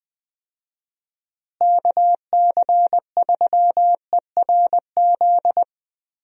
Morse code, heard as a keyed tone, about 20 wpm, 700 Hz. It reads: KC3ERZ
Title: USA Call Signs - 6 Characters - Repeated 3x - 20wpm